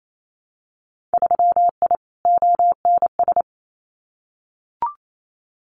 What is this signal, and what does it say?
Morse code, heard as a keyed tone, about 28 wpm, 700 Hz. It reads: 3I ONS
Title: Winter Field Day Exchanges Encore 28wpm